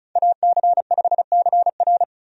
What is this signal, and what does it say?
Morse code, heard as a keyed tone, about 36 wpm, 700 Hz. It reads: AC5CR
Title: USA Call Signs - 5 Characters - 35wpm